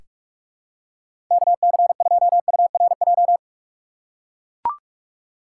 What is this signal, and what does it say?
Morse code, heard as a keyed tone, about 42 wpm, 690 Hz. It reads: KC2URJ
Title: USA Call Signs - 6 Characters - 2x Character Spacing - 45wpm